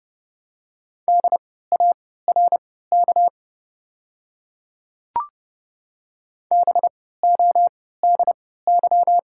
Morse code, heard as a keyed tone, about 32 wpm, spaced as for 17 wpm, 700 Hz. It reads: DARK BODY
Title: 3x Character Spacing - Top 500 Words - 30wpm